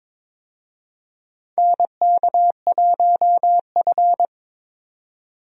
Morse code, heard as a keyed tone, about 22 wpm, 700 Hz. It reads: NK1F
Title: USA Call Signs - 4 Characters - 22wpm